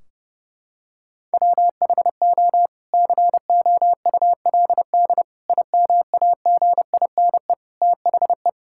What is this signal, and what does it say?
Morse code, heard as a keyed tone, about 30 wpm, 700 Hz. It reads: WHO COULD IMAGINE THE
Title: Sentences from Top 1600 Words 30wpm